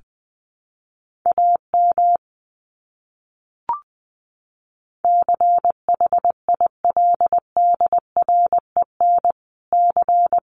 Morse code, heard as a keyed tone, about 20 wpm, 700 Hz. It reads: AM CHILDREN C
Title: Top 500 Words Repeated 20wpm